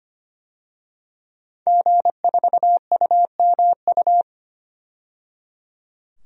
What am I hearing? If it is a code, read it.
Morse code, heard as a keyed tone, about 25 wpm, 700 Hz. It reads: G4UMU
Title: DXCC-100 Call Signs - 5 Characters - 25wpm